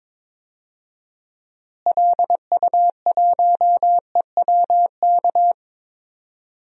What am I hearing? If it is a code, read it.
Morse code, heard as a keyed tone, about 22 wpm, 700 Hz. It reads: LU1EWK
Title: DXCC-100 Call Signs - 6 Characters - Repeated 3x - 22wpm